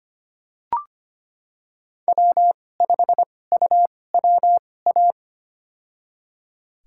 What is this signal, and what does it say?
Morse code, heard as a keyed tone, about 25 wpm, spaced as for 18 wpm, 700 Hz. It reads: W5UWA
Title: USA Call Signs - 5 Characters - 2x Character Spacing - 25wpm